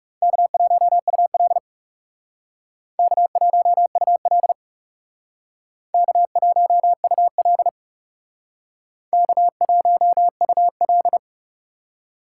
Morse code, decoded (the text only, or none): K1UL K1UL K1UL K1UL